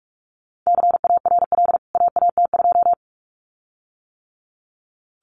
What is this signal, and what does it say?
Morse code, heard as a keyed tone, about 44 wpm, 700 Hz. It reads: CARL AAT2
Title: CWT Exchange 45wpm